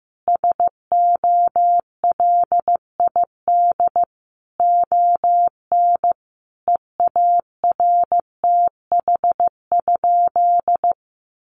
Morse code, heard as a keyed tone, about 15 wpm, 700 Hz. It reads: SOLID ON EARTH?
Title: Sentences from Top 1500 Words 15wpm